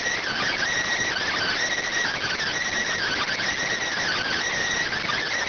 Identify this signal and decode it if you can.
Emergency Alert System (SAME) tones (header bursts), heard as harsh buzzing bursts. It is heard from 0.4 to 4.9 s.